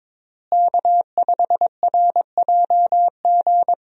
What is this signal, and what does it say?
Morse code, heard as a keyed tone, about 22 wpm, 700 Hz. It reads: K5RJG
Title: USA Call Signs - 5 Characters - Repeated 3x - 22wpm